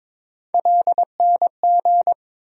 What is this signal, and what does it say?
Morse code, heard as a keyed tone, about 22 wpm, 700 Hz. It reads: LNG
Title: USA Call Sign Postfixes - 3 Letters 22wpm